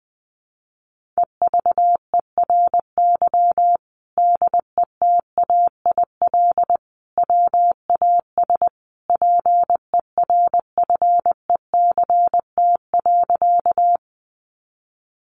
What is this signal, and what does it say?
Morse code, heard as a keyed tone, about 20 wpm, 700 Hz. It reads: EVERY DETAIL WAS PERFECT.
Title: Sentences from Top 1500 Words 20wpm